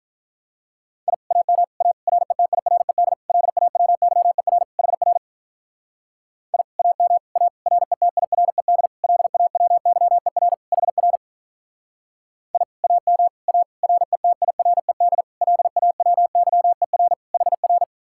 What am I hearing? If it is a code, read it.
Morse code, heard as a keyed tone, about 44 wpm, 700 Hz. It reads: I AM A RETIRED LAWYER HR I AM A RETIRED LAWYER HR I AM A RETIRED LAWYER HR